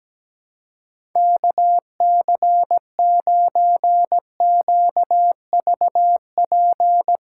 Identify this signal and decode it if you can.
Morse code, heard as a keyed tone, about 17 wpm, 700 Hz. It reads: KC9QVP
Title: USA Callsign DE Callsign - 6 Characters - 17wpm